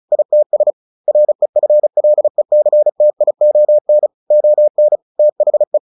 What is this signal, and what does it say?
Morse code, heard as a keyed tone, about 34 wpm, 600 Hz. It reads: ITS REFLECTION ON THE
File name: the old man and the sea / the old man and the sea 1 35wpm